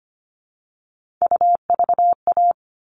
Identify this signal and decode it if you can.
Morse code, heard as a keyed tone, about 25 wpm, 700 Hz. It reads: UVA
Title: All Cut Numbers - 3-Digit 25wpm